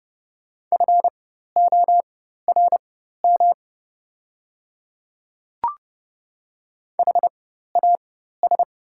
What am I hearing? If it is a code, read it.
Morse code, heard as a keyed tone, about 31 wpm, spaced as for 14 wpm, 700 Hz. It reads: FORM HAS
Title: 4x Character Spacing - Top 200 Words - 30wpm